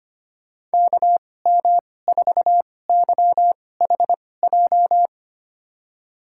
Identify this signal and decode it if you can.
Morse code, heard as a keyed tone, about 25 wpm, spaced as for 18 wpm, 700 Hz. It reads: KM4YHJ
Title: USA Call Signs - 2x Character Spacing - 25wpm